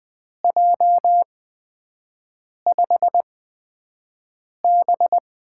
Morse code, heard as a keyed tone, about 20 wpm, spaced as for 5 wpm, 700 Hz. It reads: J5B
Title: USA Call Signs - 8x Character Spacing - 20wpm